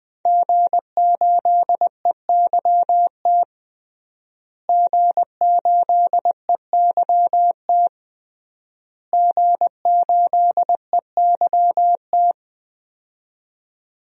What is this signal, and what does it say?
Morse code, heard as a keyed tone, about 20 wpm, 700 Hz. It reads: G8EYT G8EYT G8EYT
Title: DXCC-100 Call Signs - 5 Characters - Repeated 3x - 20wpm